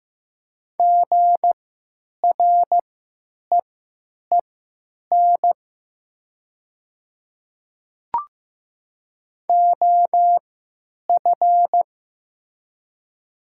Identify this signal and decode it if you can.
Morse code, heard as a keyed tone, about 16 wpm, spaced as for 9 wpm, 700 Hz. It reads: GREEN OF
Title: 3x Character Spacing - Top 500 Words - 15wpm